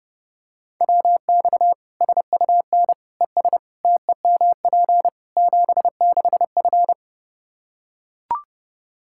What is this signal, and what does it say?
Morse code, heard as a keyed tone, about 30 wpm, 700 Hz. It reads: WX SUN ES TEMP 76F
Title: Practice QSO - Exchange 2 - Info - 30wpm